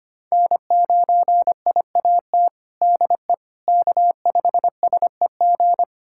Morse code, heard as a keyed tone, about 25 wpm, 700 Hz. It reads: N9IAT DE K5SEG
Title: USA Callsign DE Callsign - 5 Characters - 25wpm